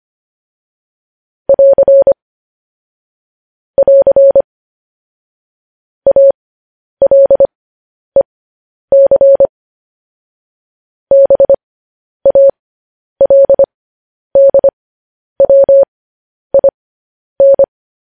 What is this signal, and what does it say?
Morse code, heard as a keyed tone, about 25 wpm, spaced as for 10 wpm, 550 Hz. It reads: <AR> <AR> ALEC BALDWIN